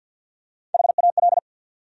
Morse code, heard as a keyed tone, about 49 wpm, 700 Hz. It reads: HAL
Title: USA Common Names - Max 5 Letters - 50wpm